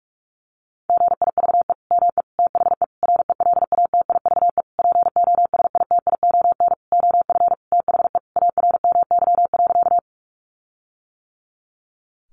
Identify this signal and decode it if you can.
Morse code, heard as a keyed tone, about 44 wpm, 700 Hz. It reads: GIVE ME THE RELATIVE POSITION OF THE ARMY.
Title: Sentences from Top 1400 Words 45wpm